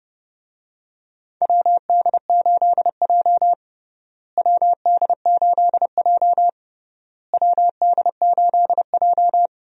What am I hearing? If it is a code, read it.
Morse code, heard as a keyed tone, about 30 wpm, 700 Hz. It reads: WD8J WD8J WD8J
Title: USA Call Signs - 4 Characters - Repeated 3x - 30wpm